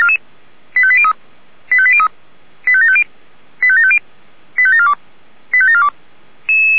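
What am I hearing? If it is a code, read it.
Five-tone selective call (ZVEI), heard as quick runs of five tones, tones of about 70 ms. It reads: 85803 85803 85750 85750 85742 85742
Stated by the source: Five-tone ZVEI selective call 85749, received by SDR.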